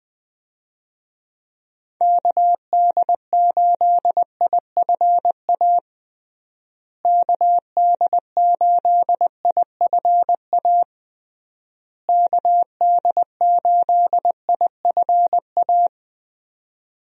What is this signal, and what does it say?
Morse code, heard as a keyed tone, about 20 wpm, 700 Hz. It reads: KD8IFA KD8IFA KD8IFA
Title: USA Call Signs - 6 Characters - Repeated 3x - 20wpm